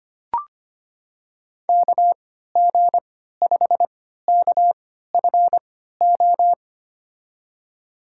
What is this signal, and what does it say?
Morse code, heard as a keyed tone, about 25 wpm, spaced as for 14 wpm, 700 Hz. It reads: KG5KFO
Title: USA Call Signs - 6 Characters - 3x Character Spacing - 25wpm